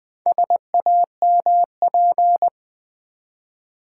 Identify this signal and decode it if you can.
Morse code, heard as a keyed tone, about 20 wpm, 700 Hz. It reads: SAMP
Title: Build A Word - 2 Letters At A Time - 6-Letter Words - 20wpm